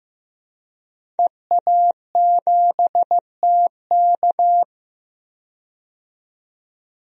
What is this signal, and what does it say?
Morse code, heard as a keyed tone, about 16 wpm, 700 Hz. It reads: EA7TK
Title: DXCC-100 Call Signs - 5 Characters - 15wpm